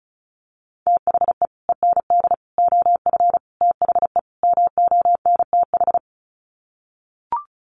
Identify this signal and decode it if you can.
Morse code, heard as a keyed tone, about 34 wpm, 700 Hz. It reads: THE END OF THE MONTH
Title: Sets of 5 Words Encore 35wpm